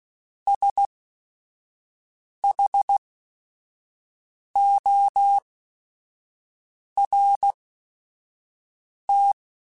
Morse code, heard as a keyed tone, about 16 wpm, spaced as for 5 wpm, 800 Hz. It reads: SHORT
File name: Technical Subjects / Technical Subjects-mp3s / XC @5